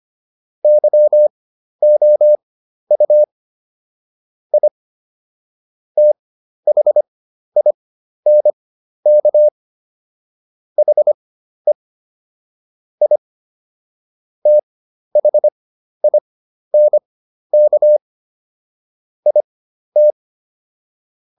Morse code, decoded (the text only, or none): YOU I THINK HE I THINK IT